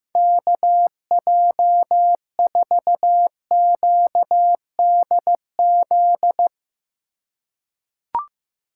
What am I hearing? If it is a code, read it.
Morse code, heard as a keyed tone, about 15 wpm, 700 Hz. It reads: KJ4QDZ
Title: USA Call Signs - 6 Characters - 4x Character Spacing - 15wpm